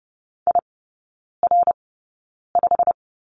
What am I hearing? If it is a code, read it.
Morse code, heard as a keyed tone, about 30 wpm, spaced as for 9 wpm, 700 Hz. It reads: IR5
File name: Lesson 17e - Letter u - Three Characters 30wpm